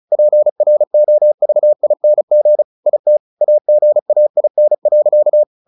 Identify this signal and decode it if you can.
Morse code, heard as a keyed tone, about 34 wpm, 600 Hz. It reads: PROVING IT AGAIN.
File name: the old man and the sea / the old man and the sea 1 35wpm